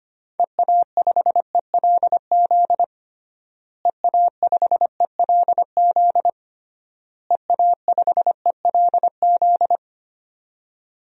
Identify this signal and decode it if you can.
Morse code, heard as a keyed tone, about 25 wpm, 700 Hz. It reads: EA5ELZ EA5ELZ EA5ELZ
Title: DXCC-100 Call Signs - 6 Characters - Repeated 3x - 25wpm